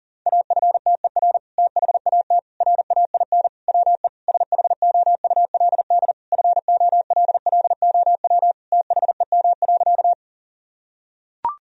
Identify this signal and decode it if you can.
Morse code, heard as a keyed tone, about 40 wpm, 700 Hz. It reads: AFTER THAT RAIN WE SHOULD FOLLOW THEM.
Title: Sentences from Top 1000 Words - Review Part 1 - 40wpm